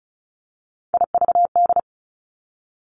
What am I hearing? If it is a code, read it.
Morse code, heard as a keyed tone, about 36 wpm, 700 Hz. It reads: IVD
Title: Three Challenging Letter-Number Combinations 35wpm